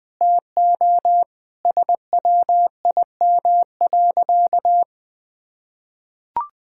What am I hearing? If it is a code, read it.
Morse code, heard as a keyed tone, about 20 wpm, 700 Hz. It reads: TO SWIM.
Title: Sentences with 4-Word Phrases - 20wpm